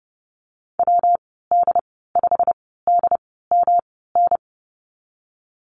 Morse code, heard as a keyed tone, about 31 wpm, 700 Hz. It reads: W D 5 D M N
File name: Lesson 19g - Letter w - Call Signs 30wpm